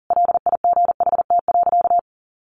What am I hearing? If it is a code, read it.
Morse code, heard as a keyed tone, about 37 wpm, 700 Hz. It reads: LIGHT.
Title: Sentences from Top 400 Words 40wpm